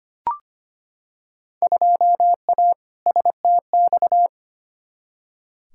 Morse code, heard as a keyed tone, about 25 wpm, 700 Hz. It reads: 2A STX
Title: Field Day Exchanges - 25wpm